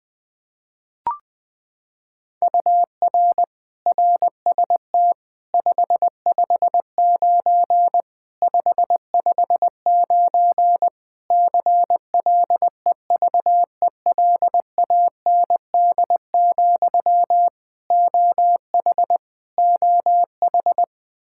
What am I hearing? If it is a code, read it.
Morse code, heard as a keyed tone, about 20 wpm, 700 Hz. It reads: UR RST 559 559 CLEVELAND, OH OH
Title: Practice QSO - Exchange 1 - Beginning - 20wpm